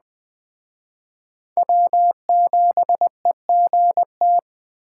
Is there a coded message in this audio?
Yes: Morse code from 1.6 to 4.4 s, heard as a keyed tone, about 20 wpm, 700 Hz. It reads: W7EGT